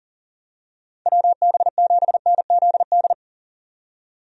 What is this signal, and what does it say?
Morse code, heard as a keyed tone, about 38 wpm, 700 Hz. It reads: WB7NZD
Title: USA Call Signs - 6 Characters - 40wpm